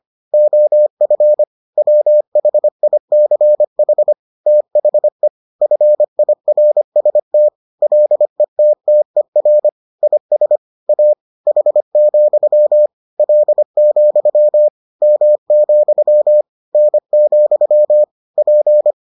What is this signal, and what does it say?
Morse code, heard as a keyed tone, about 25 wpm, 600 Hz. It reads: OF WHICH THE FIRST LETTER IS A H, L, M, N, P